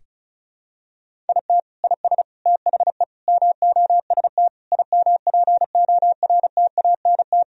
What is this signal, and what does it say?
Morse code, heard as a keyed tone, about 35 wpm, 700 Hz. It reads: IT IS THE MOST IMPORTANT
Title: Sentences from Top 1000 Words - Review Part 1 - 35wpm